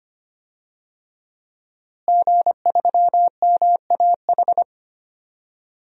Morse code, heard as a keyed tone, about 25 wpm, 700 Hz. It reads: G3MAH
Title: DXCC-100 Call Signs - 5 Characters - Repeated 2x - 25wpm